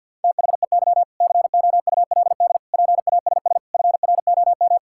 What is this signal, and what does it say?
Morse code, heard as a keyed tone, about 50 wpm, 700 Hz. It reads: THEY COULD PASS FROM
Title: Sentences from Top 1000 Words - Review Part 6 - 50wpm